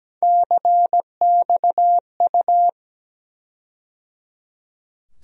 Morse code, heard as a keyed tone, about 17 wpm, 700 Hz. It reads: CXU
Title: USA Call Sign Postfixes - 3 Letters 17wpm